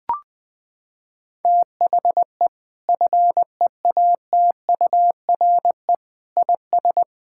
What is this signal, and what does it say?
Morse code, heard as a keyed tone, about 19 wpm, 700 Hz. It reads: THE FEATURE IS
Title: Sentences from Top 1000 Words - Review Part 5 - 20wpm